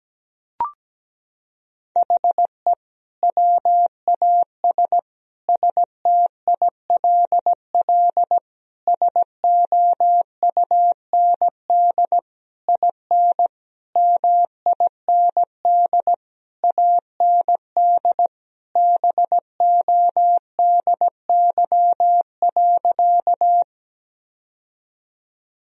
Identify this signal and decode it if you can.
Morse code, heard as a keyed tone, about 17 wpm, 700 Hz. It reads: HE WAS STILL SOUND IN MIND AND BODY.
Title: Sentences from Top 1000 Words - Review Part 2 - 17wpm